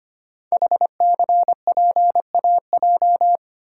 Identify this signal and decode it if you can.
Morse code, heard as a keyed tone, about 25 wpm, 700 Hz. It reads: HCPAJ
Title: Code Groups - 5 Letters - 25wpm